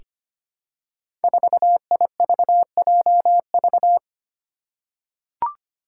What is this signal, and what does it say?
Morse code, heard as a keyed tone, about 25 wpm, 700 Hz. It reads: 4IVJV
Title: Five Challenging Letter-Number Combinations 25wpm